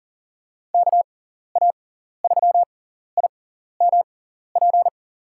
Morse code, heard as a keyed tone, about 40 wpm, spaced as for 14 wpm, 700 Hz. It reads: KA3IMP
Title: USA Call Signs - 6 Characters - 6x Character Spacing - 40wpm